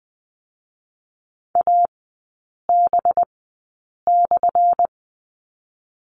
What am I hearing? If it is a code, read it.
Morse code, heard as a keyed tone, about 20 wpm, spaced as for 8 wpm, 700 Hz. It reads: AB/
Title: 184 Lesson 30e - Letter b - Three Characters 20wpm